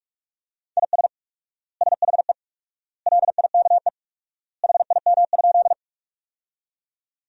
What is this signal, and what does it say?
Morse code, heard as a keyed tone, about 41 wpm, 700 Hz. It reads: IS SHE LIKE HIM?
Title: 4x Word-Spacing - Sentences from Top 100 Words - 45wpm